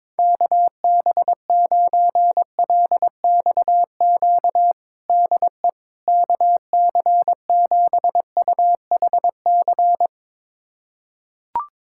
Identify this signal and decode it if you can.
Morse code, heard as a keyed tone, about 22 wpm, 700 Hz. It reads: KB9LXQ DE KC7UHC
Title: USA Callsign DE Callsign - 6 Characters - 22wpm